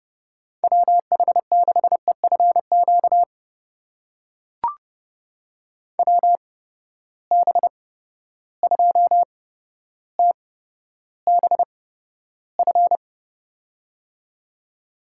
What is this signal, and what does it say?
Morse code, heard as a keyed tone, about 30 wpm, spaced as for 13 wpm, 700 Hz. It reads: WH6EFQ WB2TBF
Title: USA Call Signs - 8x Character Spacing - 30wpm